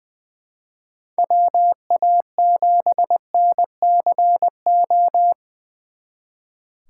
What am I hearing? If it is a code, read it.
Morse code, heard as a keyed tone, about 24 wpm, 700 Hz. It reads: WA7NCO
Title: Speed Racing - USA Call Signs 20wpm